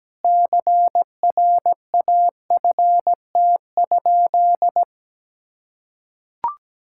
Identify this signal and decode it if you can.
Morse code, heard as a keyed tone, about 17 wpm, 700 Hz. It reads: CRAFT?
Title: Sentences from Top 1000 Words - Review Part 6 - 17wpm